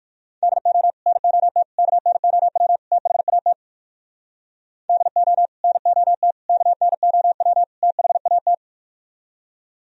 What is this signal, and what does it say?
Morse code, heard as a keyed tone, about 65 wpm, spaced as for 56 wpm, 700 Hz. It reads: DO NOT KNOW THAT DO NOT KNOW THAT